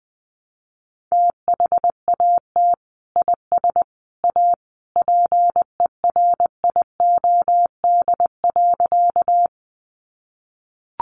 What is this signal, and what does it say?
Morse code, heard as a keyed tone, about 20 wpm, 700 Hz. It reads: THAT IS A PERIOD.
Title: Sentences from Top 800 Words 20wpm